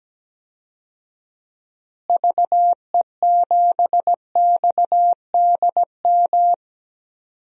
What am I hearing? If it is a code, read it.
Morse code, heard as a keyed tone, about 17 wpm, 700 Hz. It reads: VE7XDM
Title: DXCC-100 Call Signs - 6 Characters - Repeated 3x - 17wpm